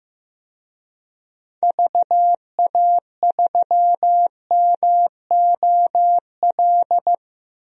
Morse code, heard as a keyed tone, about 15 wpm, 700 Hz. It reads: VA3MOL
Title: DXCC-100 Call Signs - 6 Characters - 15wpm